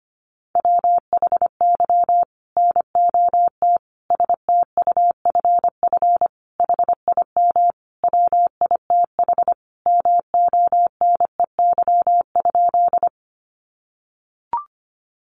Morse code, heard as a keyed tone, about 25 wpm, 700 Hz. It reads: WHY NOT STUFF HIM WITH MONEY?
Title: Sentences from Top 1400 Words 25wpm